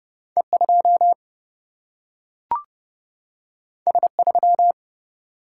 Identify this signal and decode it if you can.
Morse code, heard as a keyed tone, about 30 wpm, 700 Hz. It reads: E2 S3
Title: Single-Letter Single-Number 30wpm